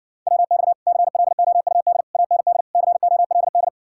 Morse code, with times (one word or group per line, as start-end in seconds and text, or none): 0.3-0.7 WX
0.9-2.0 CLOUD
2.1-2.6 AND
2.8-3.7 COLD